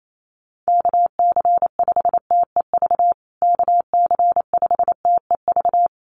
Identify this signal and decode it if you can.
Morse code, heard as a keyed tone, about 28 wpm, 700 Hz. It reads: KC5TEV KC5TEV
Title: USA Call Signs Repeated 2x 28wpm